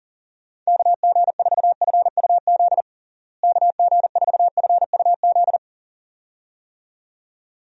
Morse code, heard as a keyed tone, about 40 wpm, 700 Hz. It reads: KG4FUZ KG4FUZ
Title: USA Call Signs - 6 Characters - Repeated 2x - 40wpm